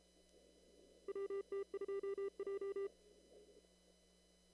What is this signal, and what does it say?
Morse code, heard as a keyed tone, about 34 wpm, 400 Hz. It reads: WT2J
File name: CW / kt5j wpx cw-op-k5pi